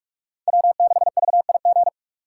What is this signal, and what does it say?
Morse code, heard as a keyed tone, about 46 wpm, 700 Hz. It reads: W6VIG
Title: USA Call Signs - 5 Characters - 3x Character Spacing - 45wpm